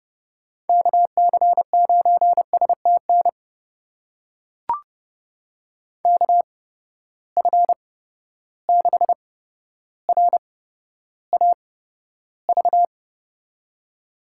Morse code, heard as a keyed tone, about 31 wpm, spaced as for 13 wpm, 700 Hz. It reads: KC9STN KF6RAV
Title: USA Call Signs - 6 Characters - 8x Character Spacing - 30wpm